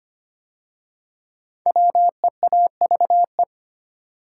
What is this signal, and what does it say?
Morse code, heard as a keyed tone, about 25 wpm, 700 Hz. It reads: WEAVE
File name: 5 Letter Words - First 3 Letters Given - 25wpm